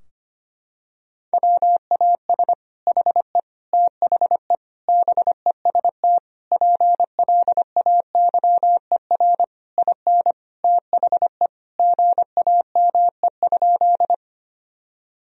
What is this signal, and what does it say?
Morse code, heard as a keyed tone, about 25 wpm, 700 Hz. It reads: WAS HE THE BEST PLAYER IN THE GAME?
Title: Sentences from Top 1000 Words - Review Part 4 - 25wpm